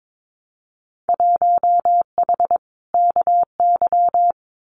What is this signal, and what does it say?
Morse code, heard as a keyed tone, about 22 wpm, 700 Hz. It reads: 1H KY
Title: Winter Field Day Exchanges 22wpm